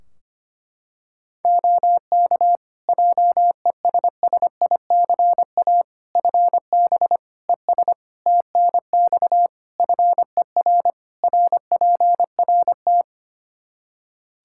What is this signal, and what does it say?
Morse code, heard as a keyed tone, about 25 wpm, 700 Hz. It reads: OK JESSICA FB ES TNX FER RPRT
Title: Practice QSO - Exchange 2 - Info - 25wpm